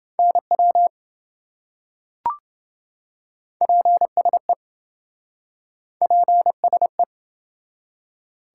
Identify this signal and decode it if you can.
Morse code, heard as a keyed tone, about 27 wpm, 700 Hz. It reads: NW PSE PSE